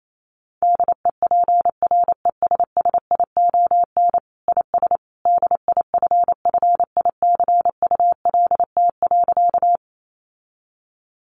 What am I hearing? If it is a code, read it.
Morse code, heard as a keyed tone, about 28 wpm, 700 Hz. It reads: DEPRESSION IS DIFFICULT.
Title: Sentences from Top 1300 Words 28wpm